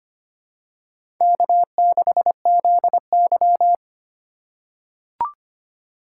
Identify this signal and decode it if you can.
Morse code, heard as a keyed tone, about 25 wpm, 700 Hz. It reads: K6ZY
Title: USA Call Signs - 4 Characters - 3x Character Spacing - 25wpm